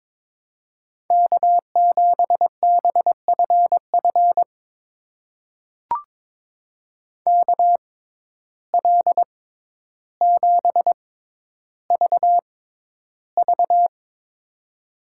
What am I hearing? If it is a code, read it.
Morse code, heard as a keyed tone, about 22 wpm, spaced as for 11 wpm, 700 Hz. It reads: K7BFF KL7VV
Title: USA Call Signs - 5 Characters - 6x Character Spacing - 22wpm